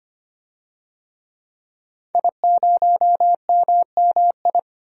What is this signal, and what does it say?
Morse code, heard as a keyed tone, about 25 wpm, 700 Hz. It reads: I0MMI
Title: DXCC-100 Call Signs - 5 Characters - Repeated 3x - 25wpm